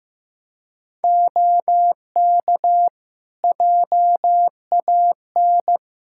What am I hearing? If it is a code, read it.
Morse code, heard as a keyed tone, about 16 wpm, 700 Hz. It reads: OK JAN
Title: K1USN SST QSO Practice 15wpm